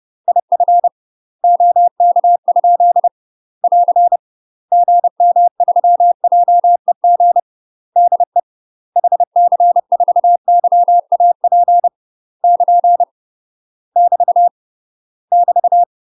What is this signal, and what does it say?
Morse code, heard as a keyed tone, about 30 wpm, 700 Hz. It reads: IF OK? <AR> GM3JEG DE HC4YAP <KN> <BT> <BT>
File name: UK MASTER QSOs TRAINING / 30WPM / UK MASTER QSOs 0016